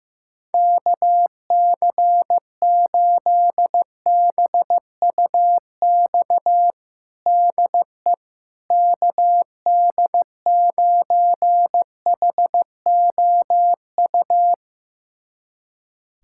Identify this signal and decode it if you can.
Morse code, heard as a keyed tone, about 16 wpm, 700 Hz. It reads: KC8BUX DE KD9HOU
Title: USA Callsign DE Callsign - 6 Characters - 15wpm